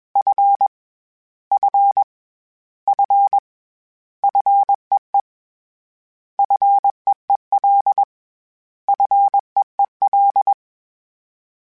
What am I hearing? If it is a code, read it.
Morse code, heard as a keyed tone, about 21 wpm, 800 Hz. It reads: F F F FEE FEEL FEEL
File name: Short Runs / Short Runs-mp3s / SinglePrefixes @20